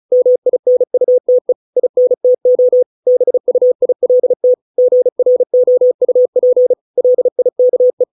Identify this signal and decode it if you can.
Morse code, heard as a keyed tone, about 34 wpm, 500 Hz. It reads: MINUTE INTO BUILT GROUP LIKE